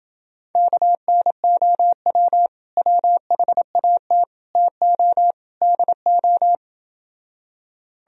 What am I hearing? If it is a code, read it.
Morse code, heard as a keyed tone, about 30 wpm, 700 Hz. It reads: KNOW WHAT TO DO